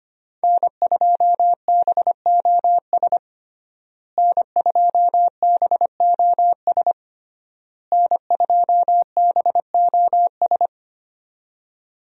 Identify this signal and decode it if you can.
Morse code, heard as a keyed tone, about 25 wpm, 700 Hz. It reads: N2BOS N2BOS N2BOS
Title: USA Call Signs - 5 Characters - Repeated 3x - 25wpm